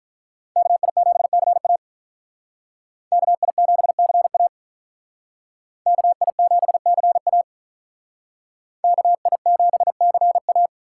Morse code, heard as a keyed tone, about 45 wpm, 700 Hz. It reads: KI7CA KI7CA KI7CA KI7CA